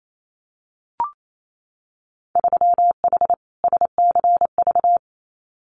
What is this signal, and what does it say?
Morse code, heard as a keyed tone, about 28 wpm, 700 Hz. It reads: 3H SCV
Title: Winter Field Day Exchanges Encore 28wpm